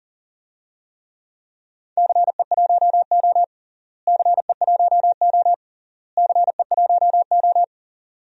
Morse code, heard as a keyed tone, about 40 wpm, 700 Hz. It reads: CE1O CE1O CE1O
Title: DXCC-100 Call Signs - 4 Characters - Repeated 3x - 40wpm